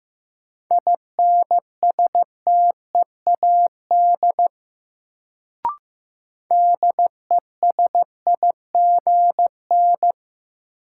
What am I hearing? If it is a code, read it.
Morse code, heard as a keyed tone, about 15 wpm, 700 Hz. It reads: INSTEAD DESIGN
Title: Sentences from Top 800 Words - New Word Drill - 15wpm